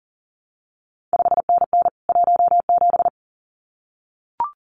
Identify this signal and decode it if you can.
Morse code, heard as a keyed tone, about 39 wpm, 700 Hz. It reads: 5NN 17
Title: CQ WW DX Contest Exchange Encore - 40wpm